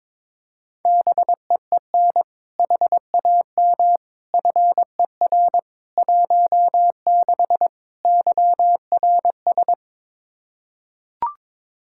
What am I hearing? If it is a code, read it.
Morse code, heard as a keyed tone, about 22 wpm, 700 Hz. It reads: BEEN HAM FER 16 YRS
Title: Practice QSO - Exchange 3 - More Info - 22wpm